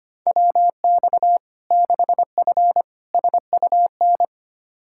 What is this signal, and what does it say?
Morse code, heard as a keyed tone, about 25 wpm, 700 Hz. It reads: WX 6F SUN
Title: Practice QSO - Exchange 2 - Info - 25wpm